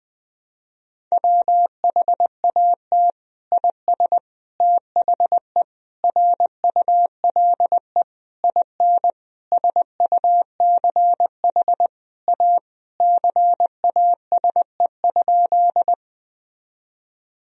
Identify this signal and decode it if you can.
Morse code, heard as a keyed tone, about 20 wpm, 700 Hz. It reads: WHAT IS THE RULE IN SUCH A CASE?
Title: Sentences from Top 1000 Words - Review Part 2 - 20wpm